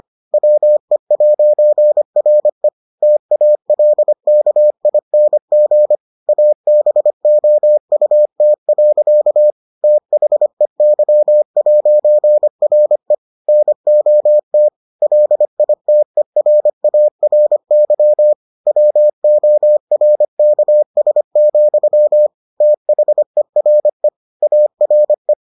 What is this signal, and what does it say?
Morse code, heard as a keyed tone, about 25 wpm, 600 Hz. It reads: WE'RE TALKING ABOUT. THEY'RE NOT LITERARY WORKS, THERE ARE